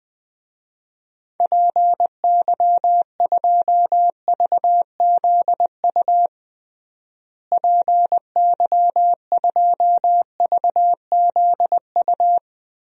Morse code, heard as a keyed tone, about 21 wpm, 700 Hz. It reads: PY2VZU PY2VZU
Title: DXCC-100 Call Signs - 6 Characters - Repeated 3x - 20wpm